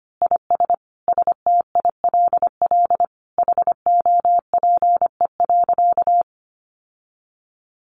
Morse code, heard as a keyed tone, about 25 wpm, 700 Hz. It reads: IS STILL HOPE.
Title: Sentences from Top 1000 Words 25wpm